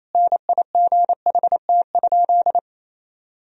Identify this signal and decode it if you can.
Morse code, heard as a keyed tone, about 28 wpm, 700 Hz. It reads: NIGHT?
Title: Sentences from Top 1000 Words - Review Part 4 - 28wpm